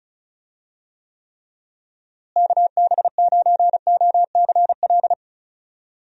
Morse code, heard as a keyed tone, about 34 wpm, 700 Hz. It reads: KB9OCL
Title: USA Call Signs - 6 Characters - Repeated 3x - 35wpm